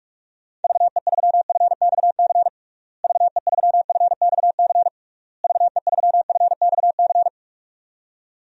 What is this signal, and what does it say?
Morse code, heard as a keyed tone, about 44 wpm, 700 Hz. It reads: VE3FXC VE3FXC VE3FXC
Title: DXCC-100 Call Signs - 6 Characters - Repeated 3x - 45wpm